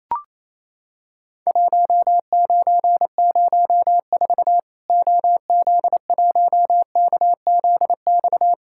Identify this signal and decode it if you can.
Morse code, heard as a keyed tone, about 28 wpm, 700 Hz. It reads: 1904 OZ1KZX
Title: Worked All Europe DX Contest - QTC Exchange - 28wpm